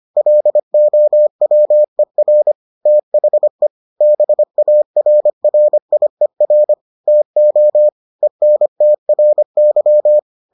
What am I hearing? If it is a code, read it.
Morse code, heard as a keyed tone, about 25 wpm, 600 Hz. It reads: LOWER THE BARRIER TO ENTRY